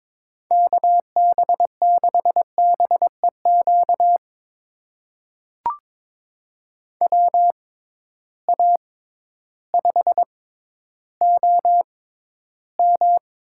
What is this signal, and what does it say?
Morse code, heard as a keyed tone, about 21 wpm, spaced as for 12 wpm, 700 Hz. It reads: KB6BEQ WA5OM
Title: USA Call Signs - 6 Characters - 6x Character Spacing - 22wpm